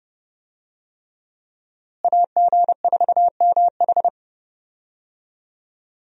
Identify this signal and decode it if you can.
Morse code, heard as a keyed tone, about 30 wpm, 700 Hz. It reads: AG4MH
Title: USA Call Signs - 5 Characters - Repeated 3x - 30wpm